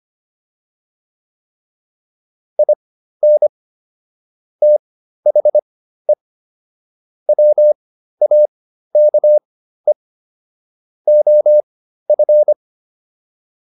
Morse code, heard as a keyed tone, about 25 wpm, spaced as for 13 wpm, 600 Hz. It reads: IN THE WAKE OF